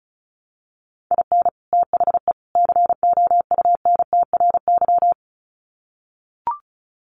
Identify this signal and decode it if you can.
Morse code, heard as a keyed tone, about 34 wpm, 700 Hz. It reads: IN THE COUNTRY
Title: Sets of 3 Words 35wpm